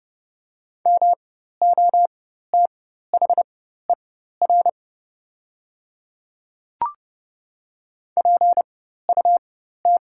Morse code, heard as a keyed tone, about 30 wpm, spaced as for 14 wpm, 700 Hz. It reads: MOTHER PUT
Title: 4x Character Spacing - Top 300 Words - 30wpm